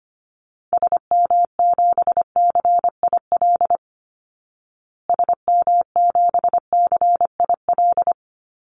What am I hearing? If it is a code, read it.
Morse code, heard as a keyed tone, about 25 wpm, 700 Hz. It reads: SM7CIL SM7CIL
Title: Call Signs Repeated 2x - 25wpm